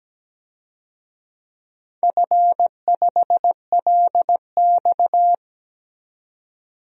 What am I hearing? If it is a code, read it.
Morse code, heard as a keyed tone, about 17 wpm, 700 Hz. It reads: F5LX
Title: DXCC-100 Call Signs - 4 Characters - 17wpm